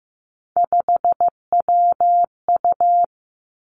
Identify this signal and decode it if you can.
Morse code, heard as a keyed tone, about 15 wpm, 700 Hz. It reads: 5WU
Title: Three Challenging Letter-Number Combinations 15wpm